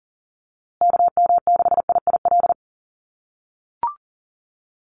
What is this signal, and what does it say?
Morse code, heard as a keyed tone, about 40 wpm, 700 Hz. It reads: KM6IIL
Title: USA Call Signs Repeated 2x 40wpm